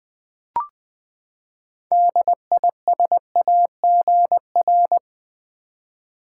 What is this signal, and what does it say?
Morse code, heard as a keyed tone, about 20 wpm, 700 Hz. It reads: DISAGR
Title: Build A Word - 6 Letters At A Time - 12-Letter Words - 20wpm